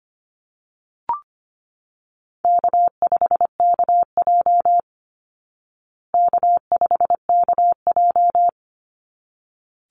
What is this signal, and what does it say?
Morse code, heard as a keyed tone, about 25 wpm, 700 Hz. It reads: K5KJ K5KJ
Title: Easy Call Signs Repeated 2x - 25wpm